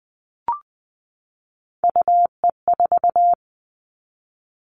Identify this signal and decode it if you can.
Morse code, heard as a keyed tone, about 20 wpm, 700 Hz. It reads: UE4
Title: Three Challenging Letter-Number Combinations 20wpm